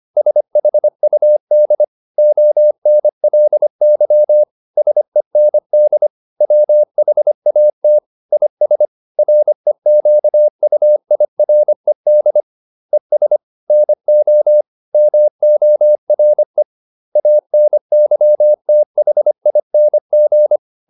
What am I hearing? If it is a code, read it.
Morse code, heard as a keyed tone, about 25 wpm, 600 Hz. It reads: SHUD ONLY SEND WHAT IS REQUIRED ES NO MORE ANYTHING